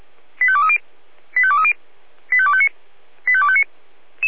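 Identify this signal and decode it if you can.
Five-tone selective call (ZVEI), heard as quick runs of five tones, tones of about 70 ms. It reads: 85230 85230 85259 85259
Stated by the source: ZVEI five-tone selective call received by SDR, from the feuerwehr-orlamuende.org fire brigade audio server.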